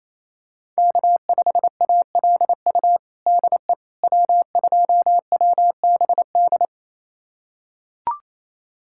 Morse code, heard as a keyed tone, about 28 wpm, 700 Hz. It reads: K5ALU DE W2WBD
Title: USA Callsign DE Callsign - 5 Characters - 28wpm